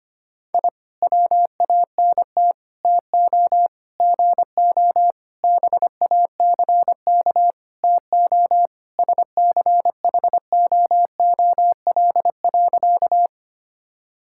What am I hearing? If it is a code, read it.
Morse code, heard as a keyed tone, about 25 wpm, 700 Hz. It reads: I WANT TO GO BACK TO SCHOOL.
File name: practice / 25wpm / Sentences with 4-Word Phrases - 25wpm